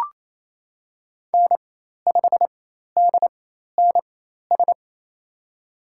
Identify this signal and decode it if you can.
Morse code, heard as a keyed tone, about 28 wpm, spaced as for 13 wpm, 700 Hz. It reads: N5DNS
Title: USA Call Signs - 4x Character Spacing - 28wpm